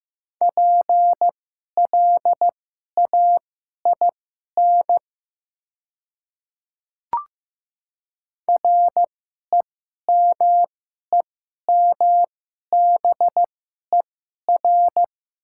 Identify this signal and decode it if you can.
Morse code, heard as a keyed tone, about 15 wpm, spaced as for 11 wpm, 700 Hz. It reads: PLAIN REMEMBER
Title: 2x Character Spacing - Top 500 Words - 15wpm